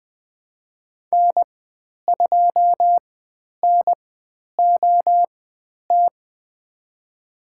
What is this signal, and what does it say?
Morse code, heard as a keyed tone, about 20 wpm, spaced as for 10 wpm, 700 Hz. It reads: N2NOT
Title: 095 Lesson 16g - Letter c - Call Signs 20wpm